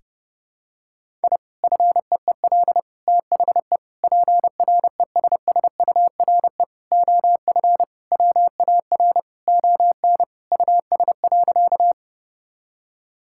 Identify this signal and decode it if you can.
Morse code, heard as a keyed tone, about 30 wpm, 700 Hz. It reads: I FEEL THE PRESSURE OF WAR ON US.
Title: Sentences from Top 1000 Words - Review Part 6 - 30wpm